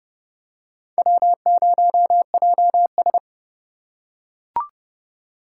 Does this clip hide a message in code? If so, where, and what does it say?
Yes: Morse code from 1.0 to 3.2 s, heard as a keyed tone, about 29 wpm, 700 Hz. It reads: W0JS